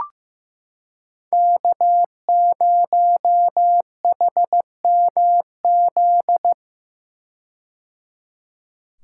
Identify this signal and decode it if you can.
Morse code, heard as a keyed tone, about 15 wpm, 700 Hz. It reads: K0HMZ
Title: K1USN SST QSO Practice - 15wpm